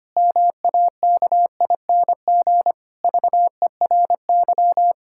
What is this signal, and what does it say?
Morse code, heard as a keyed tone, about 25 wpm, 700 Hz. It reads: MAKING VERY
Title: Pangram Challenge v2 - 25wpm